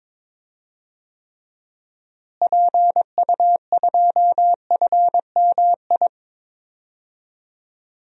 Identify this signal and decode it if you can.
Morse code, heard as a keyed tone, about 22 wpm, 700 Hz. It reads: PU2FMI
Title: DXCC-100 Call Signs - 6 Characters - Repeated 3x - 22wpm